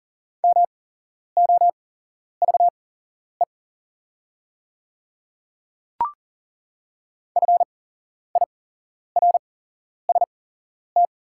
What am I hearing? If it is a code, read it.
Morse code, heard as a keyed tone, about 40 wpm, spaced as for 11 wpm, 690 Hz. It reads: MOVE FIRST
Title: 8x Character Spacing - Top 200 Words - 40wpm